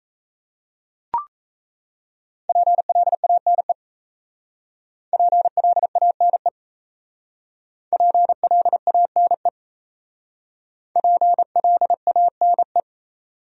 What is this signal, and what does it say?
Morse code, heard as a keyed tone, about 35 wpm, 700 Hz. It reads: PLANE PLANE PLANE PLANE
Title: Speed-Racing Top 500 Words 28wpm